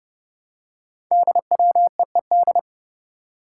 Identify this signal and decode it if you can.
Morse code, heard as a keyed tone, about 29 wpm, 700 Hz. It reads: DWEED
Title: Code Groups - 5 Letter:Numbers - 30wpm